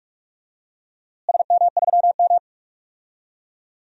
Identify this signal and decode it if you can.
Morse code, heard as a keyed tone, about 46 wpm, 700 Hz. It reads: SM3M
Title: DXCC-100 Call Signs - 4 Characters - Repeated 2x - 45wpm